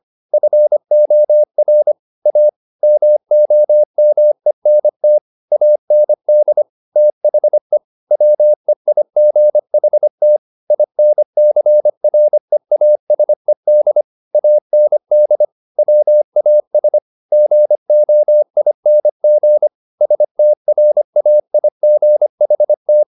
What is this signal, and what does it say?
Morse code, heard as a keyed tone, about 25 wpm, 600 Hz. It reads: FOR A MOMENT AND THE WEIGHT INCREASED AND WAS GOING STRAIGHT
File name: the old man and the sea part 1 25wpm